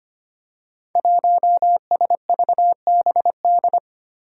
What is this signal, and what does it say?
Morse code, heard as a keyed tone, about 25 wpm, 700 Hz. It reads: 1SVBD
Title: Five Challenging Letter-Number Combinations 25wpm